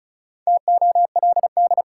Morse code, heard as a keyed tone, about 34 wpm, 700 Hz. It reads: TOLD
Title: Sentences from Top 400 Words - New Word Drill - 35wpm